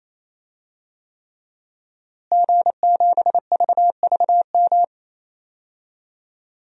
Morse code, heard as a keyed tone, about 28 wpm, 700 Hz. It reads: G7VVM
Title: DXCC-100 Call Signs - 5 Characters - Repeated 3x - 28wpm